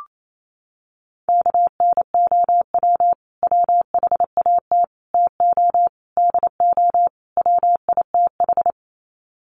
Morse code, heard as a keyed tone, about 28 wpm, 700 Hz. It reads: KNOW WHAT TO DO WITH
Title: Sets of 5 Words 28wpm